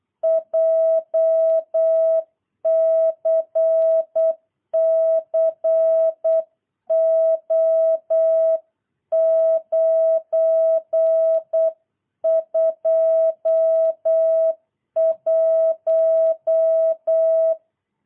Morse code, heard as a keyed tone, about 8 wpm, 650 Hz. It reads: JCCO921
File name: jh8cbh / 2022cw